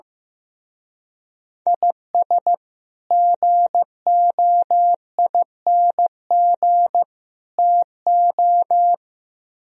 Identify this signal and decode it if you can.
Morse code, heard as a keyed tone, about 16 wpm, 700 Hz. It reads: IS GOING TO
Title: Speed-Racing - Sets of 3 Words 15wpm